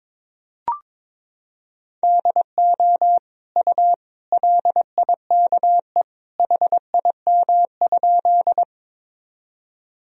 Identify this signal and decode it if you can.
Morse code, heard as a keyed tone, about 22 wpm, 700 Hz. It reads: DO U LIKE HIM?
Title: Sentences from Top 100 Words and Abbreviations 22wpm